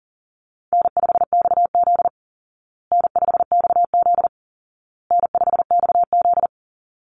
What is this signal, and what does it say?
Morse code, heard as a keyed tone, about 40 wpm, 700 Hz. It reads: N5XZ N5XZ N5XZ
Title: Easy Call Signs Repeated 3x - 40wpm